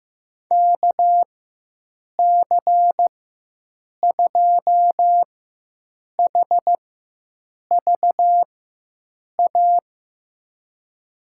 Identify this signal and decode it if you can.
Morse code, heard as a keyed tone, about 15 wpm, spaced as for 7 wpm, 700 Hz. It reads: KC2HVA
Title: USA Call Signs - 6 Characters - 4x Character Spacing - 15wpm